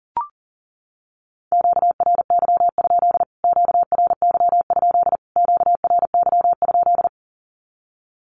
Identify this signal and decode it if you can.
Morse code, heard as a keyed tone, about 40 wpm, 700 Hz. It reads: QRY? QRY? QRY?
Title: Q-Codes Repeated 3x 40wpm